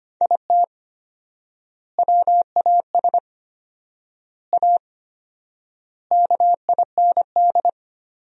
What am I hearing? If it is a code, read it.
Morse code, heard as a keyed tone, about 25 wpm, 700 Hz. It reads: IT WAS A KIND
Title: 4x Word-Spacing - Sentences from Top 500 Words - 25wpm